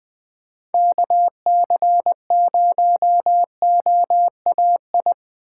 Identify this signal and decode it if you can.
Morse code, heard as a keyed tone, about 21 wpm, 700 Hz. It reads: KC0OAI